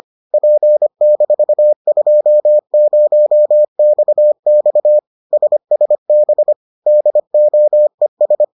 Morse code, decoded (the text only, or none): P-20XX SSB DOES